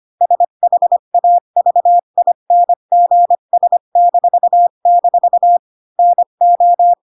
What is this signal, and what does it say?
Morse code, heard as a keyed tone, about 25 wpm, spaced as for 22 wpm, 700 Hz. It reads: SHAVINGS--NO
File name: Pinocchio / Pinocchio ch1 20 25 ik1hge